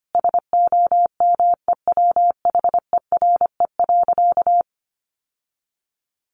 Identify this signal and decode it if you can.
Morse code, heard as a keyed tone, about 25 wpm, 700 Hz. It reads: SOMEWHERE.
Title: Sentences from Top 1400 Words 25wpm